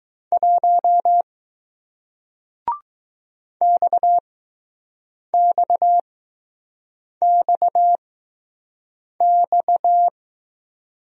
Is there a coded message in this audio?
Yes: Morse code from 0.3 to 10.1 s, heard as a keyed tone, about 20 wpm, spaced as for 6 wpm, 700 Hz. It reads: 1 XXXX